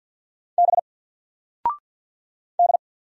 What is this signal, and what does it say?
Morse code, heard as a keyed tone, about 50 wpm, 700 Hz. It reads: B D
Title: Single Character (Letter:Number:Punctuation:Prosign) 50wpm